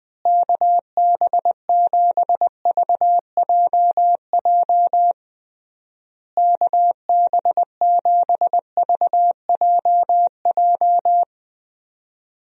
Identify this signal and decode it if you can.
Morse code, heard as a keyed tone, about 20 wpm, 700 Hz. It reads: KB7VJJ KB7VJJ
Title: USA Call Signs - 6 Characters - Repeated 3x - 20wpm